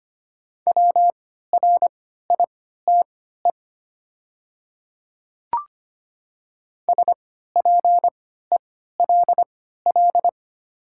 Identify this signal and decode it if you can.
Morse code, heard as a keyed tone, about 25 wpm, spaced as for 14 wpm, 700 Hz. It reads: WRITE SPELL
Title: 3x Character Spacing - Top 200 Words - 25wpm